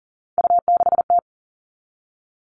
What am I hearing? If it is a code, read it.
Morse code, heard as a keyed tone, about 40 wpm, 700 Hz. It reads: U6T
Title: All Cut Numbers - 3-Digit 40wpm